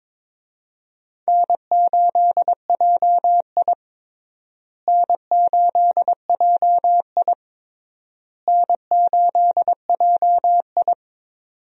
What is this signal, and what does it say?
Morse code, heard as a keyed tone, about 22 wpm, 700 Hz. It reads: N8JI N8JI N8JI
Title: USA Call Signs - 4 Characters - Repeated 3x - 22wpm